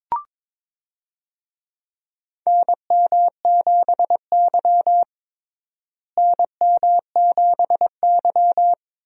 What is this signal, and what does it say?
Morse code, heard as a keyed tone, about 22 wpm, 700 Hz. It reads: NM7Y NM7Y
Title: USA Call Signs - 4 Characters - Repeated 2x - 22wpm